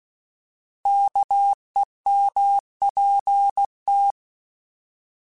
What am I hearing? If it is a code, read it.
Morse code, heard as a keyed tone, about 16 wpm, 800 Hz. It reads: KEMPT
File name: Alphabetical / Alphabetical mp3s / Ks @15